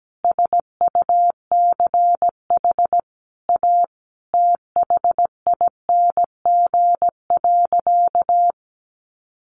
Morse code, heard as a keyed tone, about 17 wpm, 700 Hz. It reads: SUCH A THING.
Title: Sentences from Top 200 Words 17wpm